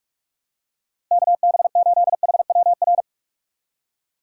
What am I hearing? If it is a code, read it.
Morse code, heard as a keyed tone, about 44 wpm, 700 Hz. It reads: KB8HWR
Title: USA Call Signs - 6 Characters - 6x Character Spacing - 45wpm